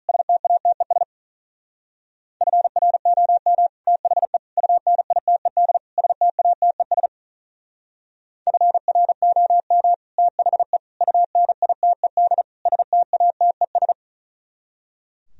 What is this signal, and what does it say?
Morse code, heard as a keyed tone, about 45 wpm, 700 Hz. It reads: STATES FROM THE UNITED STATES FROM THE UNITED STATES